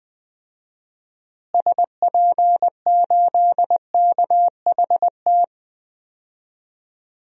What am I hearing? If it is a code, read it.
Morse code, heard as a keyed tone, about 20 wpm, 700 Hz. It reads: SP8KHT
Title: DXCC-100 Call Signs - 6 Characters - Repeated 2x - 20wpm